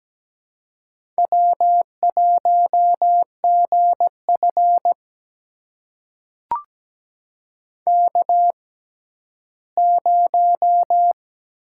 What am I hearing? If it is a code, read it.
Morse code, heard as a keyed tone, about 17 wpm, spaced as for 11 wpm, 700 Hz. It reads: W1GF K0
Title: USA Call Signs - 4 Characters - 6x Character Spacing - 17wpm